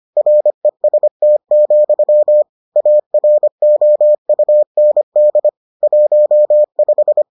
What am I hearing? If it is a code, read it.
Morse code, heard as a keyed tone, about 25 wpm, 600 Hz. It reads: REST, AROUND 15